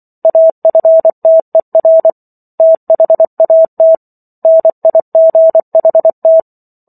Morse code, heard as a keyed tone, about 24 wpm, 650 Hz. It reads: AFTER THAT NIGHT
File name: Chapter 15 - 24 WPM